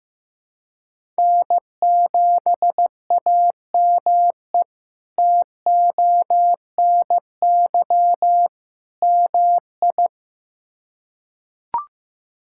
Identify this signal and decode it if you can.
Morse code, heard as a keyed tone, about 15 wpm, 700 Hz. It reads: N7AME TONY MI
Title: K1USN SST QSO Practice - 15wpm